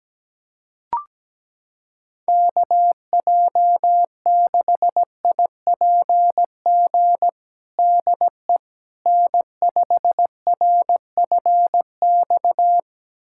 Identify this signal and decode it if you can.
Morse code, heard as a keyed tone, about 17 wpm, 700 Hz. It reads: KJ6IPG DE N5RFX
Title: USA Callsign DE Callsign 17wpm